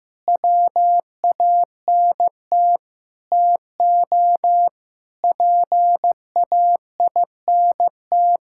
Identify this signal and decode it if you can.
Morse code, heard as a keyed tone, about 15 wpm, 700 Hz. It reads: WANT TO PAINT
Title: Sentences from Top 1000 Words - Review Part 6 - 15wpm